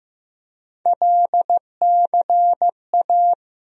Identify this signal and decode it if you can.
Morse code, heard as a keyed tone, about 15 wpm, 700 Hz. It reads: LCA
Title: USA Call Sign Postfixes - 3 Letters 15wpm